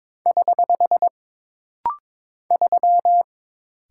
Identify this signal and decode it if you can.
Morse code, heard as a keyed tone, about 22 wpm, 700 Hz. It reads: <HH> 3
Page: This rapid-fire timing is geared to push your ICR (Instant Character Recognition) skills.